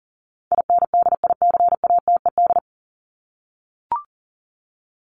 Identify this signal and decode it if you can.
Morse code, heard as a keyed tone, about 40 wpm, 700 Hz. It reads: INDICATED
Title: ED Suffix - Top 100 Words with Base Words 40wpm